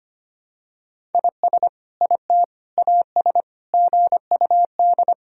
Morse code, heard as a keyed tone, about 25 wpm, 700 Hz. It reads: IS IT AS GUD
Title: Sentences from Top 200 Words and Abbreviations 25wpm